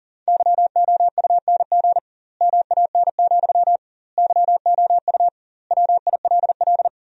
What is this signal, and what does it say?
Morse code, heard as a keyed tone, about 40 wpm, 700 Hz. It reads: YOUNG MAN, YOU WILL
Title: 2x Word-Spacing - Sentences from Top 400 Words - 40wpm